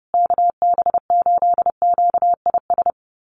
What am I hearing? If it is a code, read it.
Morse code, heard as a keyed tone, about 30 wpm, 700 Hz. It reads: KB8QIS
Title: USA Call Signs Repeated 3x 30wpm